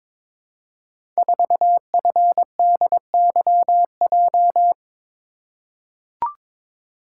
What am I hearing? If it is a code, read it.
Morse code, heard as a keyed tone, about 22 wpm, 700 Hz. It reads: 4FDYJ
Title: Code Groups - 5 Letter:Numbers - 22wpm